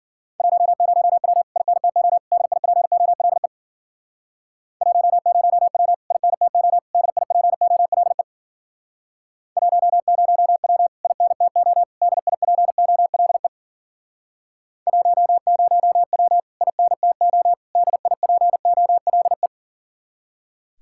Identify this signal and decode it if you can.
Morse code, heard as a keyed tone, about 54 wpm, 700 Hz. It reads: 10W INTO DIPOLE 10W INTO DIPOLE 10W INTO DIPOLE 10W INTO DIPOLE